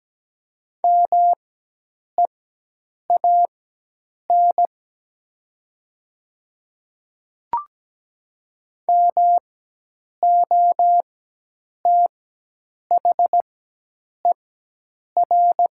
Morse code, heard as a keyed tone, about 17 wpm, spaced as for 8 wpm, 700 Hz. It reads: MEAN MOTHER
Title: 4x Character Spacing - Top 200 Words - 17wpm